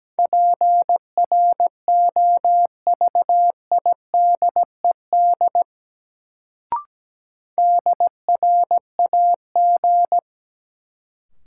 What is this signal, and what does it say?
Morse code, heard as a keyed tone, about 17 wpm, 700 Hz. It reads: PROVIDED DRAG
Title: Sentences from Top 1500 Words - New Word Drill - 17wpm